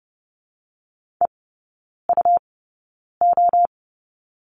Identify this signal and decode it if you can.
Morse code, heard as a keyed tone, about 30 wpm, spaced as for 9 wpm, 700 Hz. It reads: EUO
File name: Lesson 17e - Letter u - Three Characters 30wpm